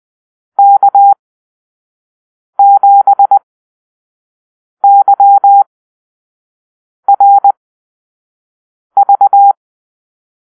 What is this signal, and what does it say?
Morse code, heard as a keyed tone, about 20 wpm, spaced as for 5 wpm, 800 Hz. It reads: K7YRV
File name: Call-Sign Drill